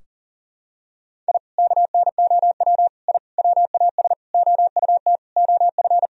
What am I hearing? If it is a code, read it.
Morse code, heard as a keyed tone, about 37 wpm, 700 Hz. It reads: I KNOW I WAS OUT OF
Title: Sentences from Top 1000 Words - Review Part 2 - 40wpm